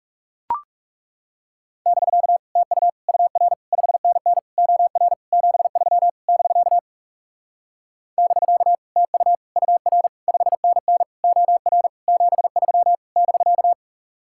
Morse code, heard as a keyed tone, about 40 wpm, 700 Hz. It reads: <BK> TU UR 5NN OR 73 <BK> <BK> TU UR 5NN OR 73 <BK>